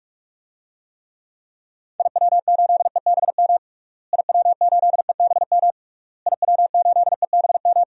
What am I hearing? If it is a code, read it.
Morse code, heard as a keyed tone, about 41 wpm, 700 Hz. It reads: IW8EBM IW8EBM IW8EBM
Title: DXCC-100 Call Signs - 6 Characters - Repeated 3x - 45wpm